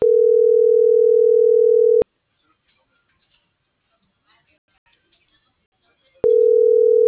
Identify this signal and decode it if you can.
Ringback tone (North America), 440 + 480 Hz.